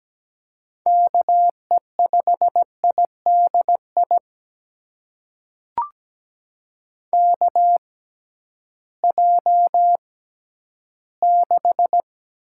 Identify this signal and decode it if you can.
Morse code, heard as a keyed tone, about 17 wpm, spaced as for 11 wpm, 700 Hz. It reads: KE5IDI KJ6
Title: USA Call Signs - 6 Characters - 6x Character Spacing - 17wpm